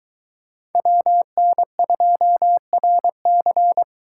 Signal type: Morse code

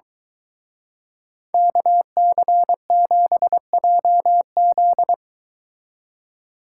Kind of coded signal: Morse code

